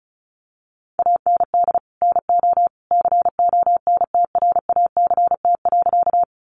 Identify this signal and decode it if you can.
Morse code, heard as a keyed tone, about 34 wpm, 700 Hz. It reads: AND NO CONTRACT.
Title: Sentences from Top 1400 Words 35wpm